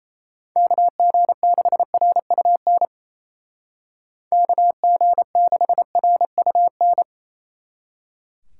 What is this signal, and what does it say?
Morse code, heard as a keyed tone, about 33 wpm, 700 Hz. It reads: KG6RUN KG6RUN
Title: Speed Racing - USA 2x3 Call Signs 28wpm